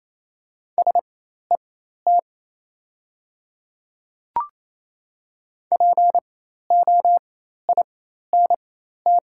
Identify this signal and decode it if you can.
Morse code, heard as a keyed tone, about 28 wpm, spaced as for 13 wpm, 700 Hz. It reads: SET POINT
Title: 4x Character Spacing - Top 300 Words - 28wpm